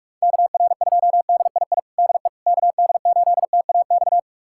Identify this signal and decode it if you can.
Morse code, heard as a keyed tone, about 46 wpm, 700 Hz. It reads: KR2DII DE KD8TAX
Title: USA Callsign DE Callsign 45wpm